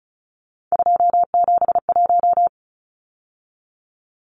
Morse code, heard as a keyed tone, about 34 wpm, 700 Hz. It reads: 271
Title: Numbers - 3-Digit 35wpm